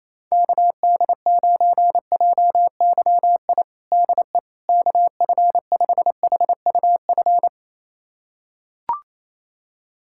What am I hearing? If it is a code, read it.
Morse code, heard as a keyed tone, about 28 wpm, 700 Hz. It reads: KD9JYI DE KF5HUF
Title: USA Callsign DE Callsign 28wpm